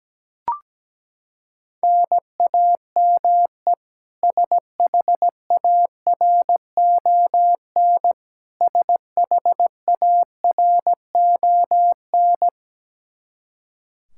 Morse code, heard as a keyed tone, about 17 wpm, 700 Hz. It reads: NAME SHARON SHARON
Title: K1USN SST QSO Practice 17wpm